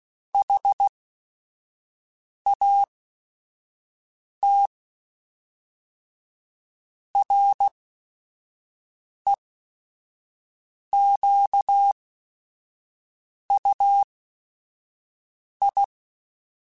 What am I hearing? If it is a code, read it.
Morse code, heard as a keyed tone, about 16 wpm, spaced as for 5 wpm, 800 Hz. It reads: HAT REQUI
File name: Technical Subjects / Technical Subjects-mp3s / AND Gate @5